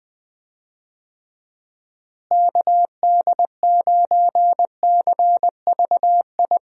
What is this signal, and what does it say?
Morse code, heard as a keyed tone, about 20 wpm, 700 Hz. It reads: KD9CVI
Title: USA Call Signs - 6 Characters - Repeated 3x - 20wpm